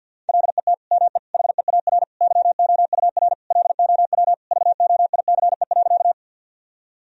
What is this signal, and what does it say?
Morse code, heard as a keyed tone, about 53 wpm, 700 Hz. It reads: LET ME HEAR YOUR LOW VOICE.
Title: Sentences from Top 1000 Words - Review Part 1 - 50wpm